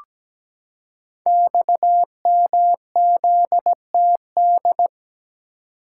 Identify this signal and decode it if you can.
Morse code, heard as a keyed tone, about 17 wpm, 690 Hz. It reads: XMZTD
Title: Code Groups - 5 Letter:Numbers - 17wpm